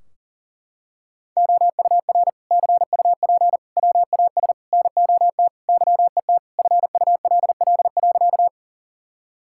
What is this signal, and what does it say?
Morse code, heard as a keyed tone, about 40 wpm, 700 Hz. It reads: OUR CUP WAS NOT YET FULL.
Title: Sentences from Top 1600 Words 40wpm